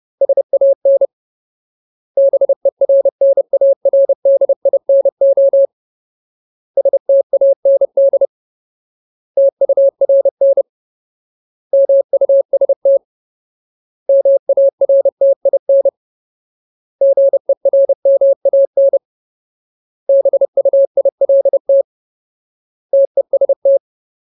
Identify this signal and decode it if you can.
Morse code, heard as a keyed tone, about 30 wpm, 550 Hz. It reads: SAN BERNARDINO STAND TURN MUST MARTIN GERMAN BUILT TEST
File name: Ultimate Word List / 30WPM Ultimate